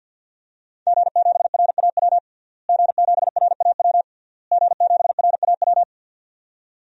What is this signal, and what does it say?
Morse code, heard as a keyed tone, about 47 wpm, 700 Hz. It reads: G7RAW G7RAW G7RAW
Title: DXCC-100 Call Signs - 5 Characters - Repeated 3x - 50wpm